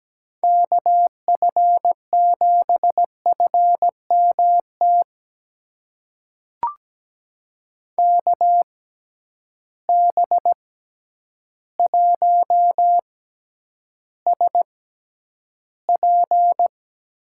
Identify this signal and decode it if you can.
Morse code, heard as a keyed tone, about 17 wpm, spaced as for 9 wpm, 700 Hz. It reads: KF7FMT KB1SP
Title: USA Call Signs - 6 Characters - 6x Character Spacing - 17wpm